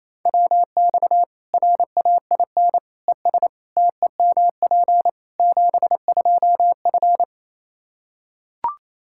Morse code, heard as a keyed tone, about 27 wpm, 700 Hz. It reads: WX RAIN ES TEMP 72F
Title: K1USN SST QSO Practice 28wpm